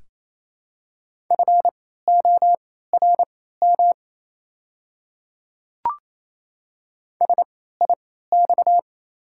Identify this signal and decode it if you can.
Morse code, heard as a keyed tone, about 28 wpm, spaced as for 16 wpm, 700 Hz. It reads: FORM SIX
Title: 3x Character Spacing - Top 500 Words - 28wpm